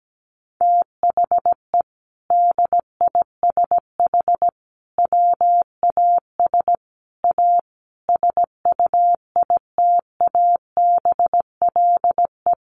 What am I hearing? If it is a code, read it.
Morse code, heard as a keyed tone, about 17 wpm, 700 Hz. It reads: THE DISH WAS A SUITABLE
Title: Sentences from Top 1400 Words 17wpm